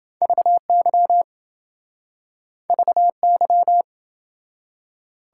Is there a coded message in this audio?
Yes: Morse code from 0.2 to 3.8 s, heard as a keyed tone, about 27 wpm, 700 Hz. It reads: VY VY